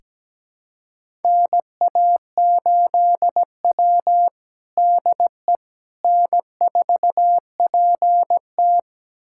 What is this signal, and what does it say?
Morse code, heard as a keyed tone, about 17 wpm, 700 Hz. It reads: NA8W DE N4PT
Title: USA Callsign DE Callsign - 4 Characters - 17wpm